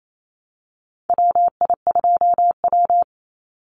Morse code, heard as a keyed tone, about 28 wpm, 700 Hz. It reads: WI2W
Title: NA SOTA Chaser Callsigns 28wpm